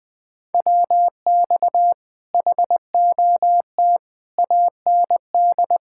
Morse code, heard as a keyed tone, about 20 wpm, 700 Hz. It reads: WX HOT AND
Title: Speed-Racing Common QSO Phrases - 17wpm